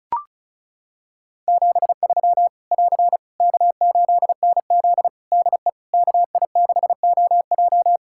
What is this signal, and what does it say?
Morse code, heard as a keyed tone, about 36 wpm, 700 Hz. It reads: 73 <AR> K8NZ DE KI6OJ
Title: Practice QSO - Exchange 4 - Ending - 35wpm